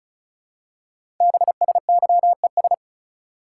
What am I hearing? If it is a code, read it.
Morse code, heard as a keyed tone, about 36 wpm, 700 Hz. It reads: BSYES
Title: Code Groups - 5 Letters - 35wpm